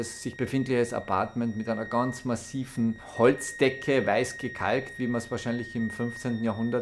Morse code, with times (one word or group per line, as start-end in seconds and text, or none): none